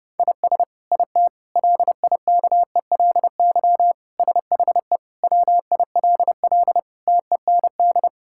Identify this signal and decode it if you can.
Morse code, heard as a keyed tone, about 30 wpm, 700 Hz. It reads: IS IT LIKELY SHE WILL TEND
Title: Sentences from Top 1000 Words - Review Part 5 - 30wpm